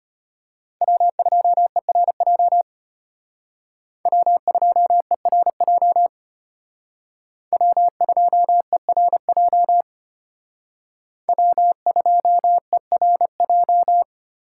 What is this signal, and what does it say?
Morse code, heard as a keyed tone, about 32 wpm, 700 Hz. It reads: W2ERJ W2ERJ W2ERJ W2ERJ